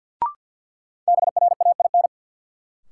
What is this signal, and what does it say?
Morse code, heard as a keyed tone, about 50 wpm, 700 Hz. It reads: BRAIN
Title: Sentences from Top 1100 Words - New Word Drill Alt - 50wpm